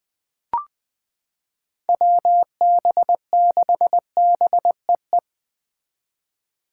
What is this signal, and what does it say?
Morse code, heard as a keyed tone, about 20 wpm, 700 Hz. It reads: WB6BEE
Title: K1USN SST QSO Practice - 20wpm